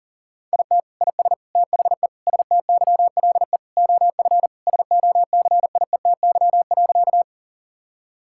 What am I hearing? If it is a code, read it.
Morse code, heard as a keyed tone, about 40 wpm, 700 Hz. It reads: IT IS THE STYLE OF SOCIETY.
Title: Sentences from Top 1000 Words - Review Part 5 - 40wpm